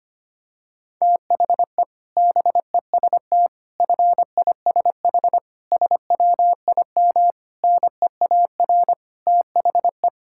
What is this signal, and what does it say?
Morse code, heard as a keyed tone, about 25 wpm, 700 Hz. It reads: THE BEST FISH SWIM NEAR THE
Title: Sentences from Top 1000 Words - Review Part 6 - 25wpm